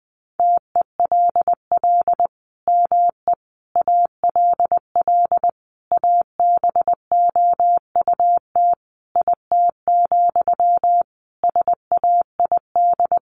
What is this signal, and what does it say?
Morse code, heard as a keyed tone, about 19 wpm, 700 Hz. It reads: TELL ME ALL ABOUT IT, SAID
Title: Sentences from Top 1500 Words 20wpm